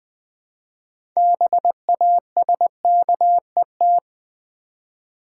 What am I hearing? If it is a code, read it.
Morse code, heard as a keyed tone, about 21 wpm, 700 Hz. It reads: BASKET
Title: Six Letter Words - 20wpm